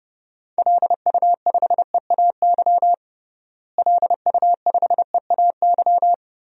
Morse code, heard as a keyed tone, about 30 wpm, 700 Hz. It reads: LU5EAY LU5EAY
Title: DXCC-100 Call Signs - 6 Characters - Repeated 3x - 30wpm